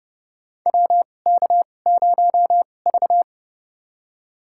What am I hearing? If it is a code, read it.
Morse code, heard as a keyed tone, about 30 wpm, spaced as for 22 wpm, 700 Hz. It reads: WK0V
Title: USA Call Signs - 2x Character Spacing - 30wpm